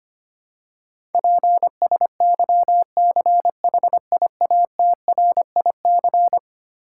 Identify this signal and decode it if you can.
Morse code, heard as a keyed tone, about 25 wpm, 700 Hz. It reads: PSYCHIATRIC
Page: morse-code-ninja-practice-25wpm